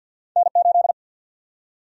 Morse code, heard as a keyed tone, about 51 wpm, 700 Hz. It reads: N8
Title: USA Call Sign Prefixes - Repeated 2x - 50wpm